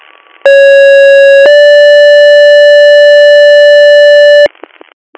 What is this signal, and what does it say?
Two-tone page, heard as a one-second tone, then a three-second tone, about 550 then 580 Hz, with noise. It is heard from 0.5 to 4.5 s.